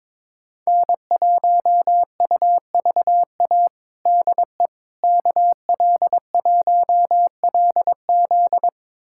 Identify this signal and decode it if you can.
Morse code, heard as a keyed tone, about 21 wpm, 700 Hz. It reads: N1UVA DE KL1LZ
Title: USA Callsign DE Callsign - 5 Characters - 22wpm